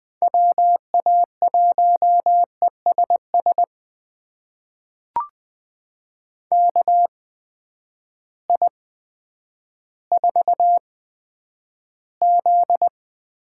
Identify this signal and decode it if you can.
Morse code, heard as a keyed tone, about 20 wpm, spaced as for 10 wpm, 700 Hz. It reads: WA1ESS KI4Z
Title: USA Call Signs - 6 Characters - 8x Character Spacing - 20wpm